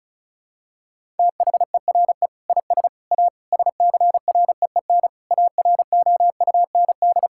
Morse code, heard as a keyed tone, about 36 wpm, 700 Hz. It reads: THERE IS A SCREEN AROUND
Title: Sentences from Top 1000 Words - Review Part 6 - 35wpm